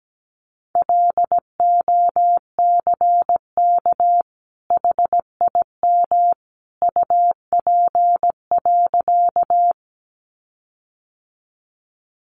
Morse code, heard as a keyed tone, about 17 wpm, 700 Hz. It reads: LOCK HIM UP.
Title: Sentences from Top 1300 Words 17wpm